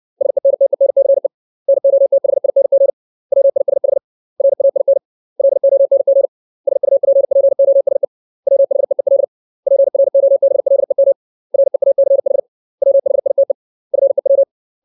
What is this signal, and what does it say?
Morse code, heard as a keyed tone, about 60 wpm, 550 Hz. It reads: SENTENCE NOTHING WISH RAIN LONG SUPPOSE WHEEL PROBLEM REACH WHITE FEW
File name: 632 words / 632 Words 60WPM